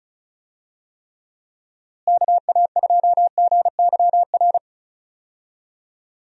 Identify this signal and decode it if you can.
Morse code, heard as a keyed tone, about 34 wpm, 700 Hz. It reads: KA2GYR
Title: USA Call Signs - 6 Characters - Repeated 3x - 35wpm